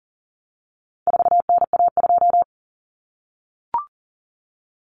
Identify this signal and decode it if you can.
Morse code, heard as a keyed tone, about 40 wpm, 700 Hz. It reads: 4NA2
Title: Common Cut Numbers - 4-Digit 40wpm